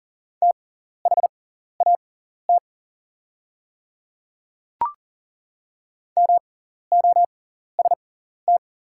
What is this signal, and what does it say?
Morse code, heard as a keyed tone, about 40 wpm, spaced as for 14 wpm, 690 Hz. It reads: THAT MOST
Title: 6x Character Spacing - Top 200 Words - 40wpm